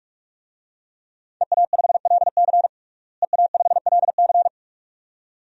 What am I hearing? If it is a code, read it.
Morse code, heard as a keyed tone, about 46 wpm, 700 Hz. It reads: EA5LC EA5LC
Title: DXCC-100 Call Signs - 5 Characters - Repeated 2x - 45wpm